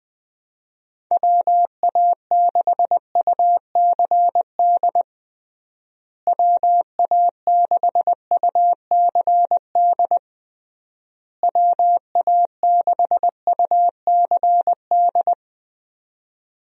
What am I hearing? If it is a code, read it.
Morse code, heard as a keyed tone, about 20 wpm, 700 Hz. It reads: WA6UCD WA6UCD WA6UCD
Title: USA Call Signs - 6 Characters - Repeated 3x - 20wpm